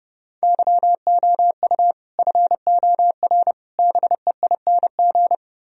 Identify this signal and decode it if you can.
Morse code, heard as a keyed tone, about 40 wpm, spaced as for 35 wpm, 700 Hz. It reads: YOU FOR BEING
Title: Speed-Racing - Sets of 5 Words 30wpm